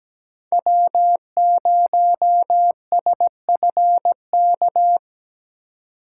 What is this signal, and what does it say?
Morse code, heard as a keyed tone, about 17 wpm, 700 Hz. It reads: W0SFK
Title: USA Call Signs - 5 Characters - 4x Character Spacing - 17wpm